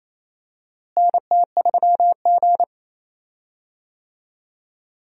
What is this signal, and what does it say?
Morse code, heard as a keyed tone, about 28 wpm, 700 Hz. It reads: NT3G
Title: USA Call Signs - 4 Characters - Repeated 3x - 28wpm